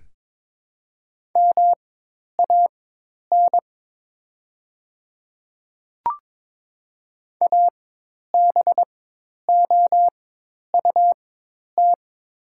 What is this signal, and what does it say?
Morse code, heard as a keyed tone, about 22 wpm, spaced as for 10 wpm, 700 Hz. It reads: MAN ABOUT
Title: 4x Character Spacing - Top 500 Words - 22wpm